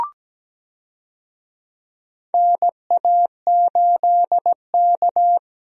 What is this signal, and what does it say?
Morse code, heard as a keyed tone, about 17 wpm, 700 Hz. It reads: NA8K
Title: USA Call Signs - 4 Characters - Repeated 3x - 17wpm